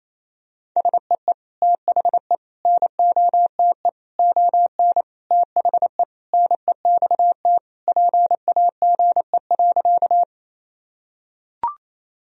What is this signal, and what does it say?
Morse code, heard as a keyed tone, about 28 wpm, 700 Hz. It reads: SEE THE NOTE ON THE NEXT PAGE.
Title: Sentences from Top 1000 Words - Review Part 2 - 28wpm